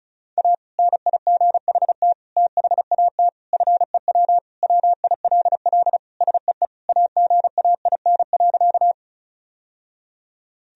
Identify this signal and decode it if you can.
Morse code, heard as a keyed tone, about 34 wpm, 700 Hz. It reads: A NIGHT THAT FEW WILL SEE AGAIN.
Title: Sentences from Top 1000 Words - Review Part 1 - 35wpm